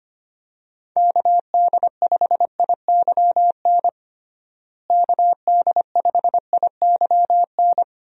Morse code, heard as a keyed tone, about 25 wpm, 700 Hz. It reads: KD5IYN KD5IYN
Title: USA Call Signs - 6 Characters - Repeated 3x - 25wpm